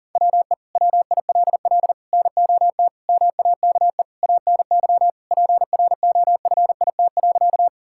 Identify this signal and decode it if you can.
Morse code, heard as a keyed tone, about 40 wpm, 700 Hz. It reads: WE WILL NOT MAKE ANY PROFIT.
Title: Sentences from Top 1000 Words - Review Part 5 - 40wpm